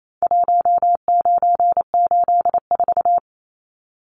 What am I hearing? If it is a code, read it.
Morse code, heard as a keyed tone, about 28 wpm, 700 Hz. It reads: 1984
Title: Numbers - Contest Style - 4-Digit 28wpm